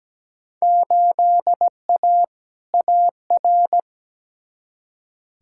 Morse code, heard as a keyed tone, about 17 wpm, 700 Hz. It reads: 8A AR
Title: Field Day Exchanges - 17wpm